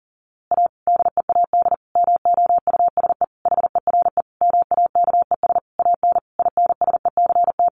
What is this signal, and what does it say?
Morse code, heard as a keyed tone, about 40 wpm, 700 Hz. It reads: A DEAD MOUSE HERE MAKES AN INSECT
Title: A Sound Of Thunder 40wpm